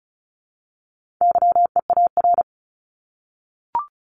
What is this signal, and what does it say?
Morse code, heard as a keyed tone, about 34 wpm, 700 Hz. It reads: YEAR
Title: Top 200 Words Repeated 35wpm